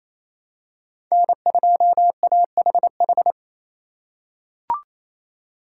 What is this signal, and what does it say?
Morse code, heard as a keyed tone, about 28 wpm, 700 Hz. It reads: N2AHH
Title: USA Call Signs - 5 Characters - 4x Character Spacing - 28wpm